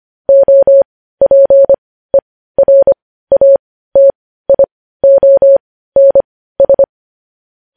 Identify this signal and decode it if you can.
Morse code, heard as a keyed tone, about 25 wpm, spaced as for 15 wpm, 550 Hz. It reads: OPERATIONS
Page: News Headlines in Morse Code at 15 WPM